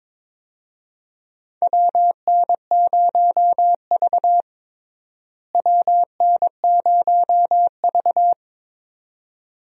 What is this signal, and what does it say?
Morse code, heard as a keyed tone, about 22 wpm, 700 Hz. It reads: WN0V WN0V
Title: USA Call Signs - 4 Characters - Repeated 2x - 22wpm